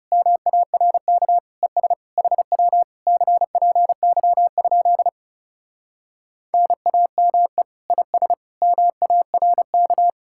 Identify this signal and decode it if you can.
Morse code, heard as a keyed tone, about 40 wpm, 700 Hz. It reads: MARK ES HW CPY? NAME IS MARK